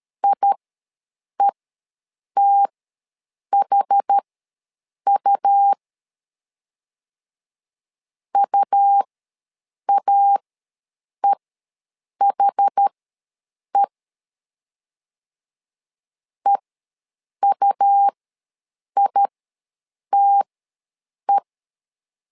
Morse code, heard as a keyed tone, about 13 wpm, spaced as for 7 wpm, 800 Hz. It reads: IETHU UAEHE EUITE
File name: LEZIONI / 2 UHDN / 2 UHD ws 12wpm 1722